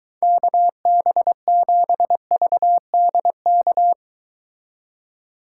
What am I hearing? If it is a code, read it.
Morse code, heard as a keyed tone, about 23 wpm, 700 Hz. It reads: KB7VDK